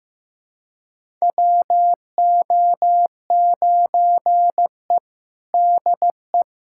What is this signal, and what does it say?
Morse code, heard as a keyed tone, about 15 wpm, 700 Hz. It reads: WO9E DE
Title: USA Callsign DE Callsign - 4 Characters - 15wpm